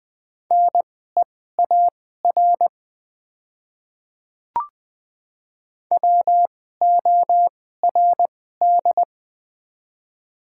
Morse code, heard as a keyed tone, about 21 wpm, spaced as for 15 wpm, 700 Hz. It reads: NEAR WORD
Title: 2x Character Spacing - Top 200 Words - 20wpm